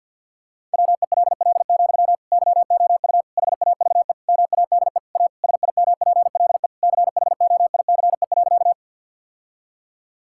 Morse code, heard as a keyed tone, about 50 wpm, 700 Hz. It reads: WELL, YOU HAVE MADE A SIMPLE CHOICE.
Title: Sentences from Top 1000 Words - Review Part 6 - 50wpm